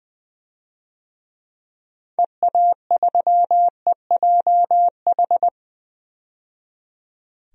Morse code, heard as a keyed tone, about 20 wpm, 700 Hz. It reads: EA3EJH
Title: DXCC-100 Call Signs - 6 Characters - 20wpm